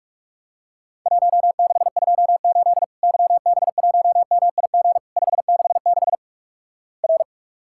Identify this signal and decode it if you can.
Morse code, heard as a keyed tone, about 47 wpm, 700 Hz. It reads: 1628 YB1MIG 566 R
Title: Worked All Europe DX Contest - QTC Exchange - 45wpm